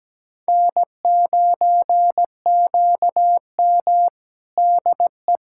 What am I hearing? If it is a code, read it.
Morse code, heard as a keyed tone, about 17 wpm, 700 Hz. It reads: N9QM DE
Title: USA Callsign DE Callsign - 4 Characters - 17wpm